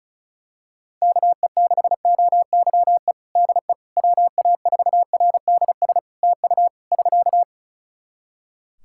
Morse code, heard as a keyed tone, about 36 wpm, 700 Hz. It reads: KE6OYE DE WA4RDS TU <SK>
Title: Practice QSO - Exchange 4 - Ending - 35wpm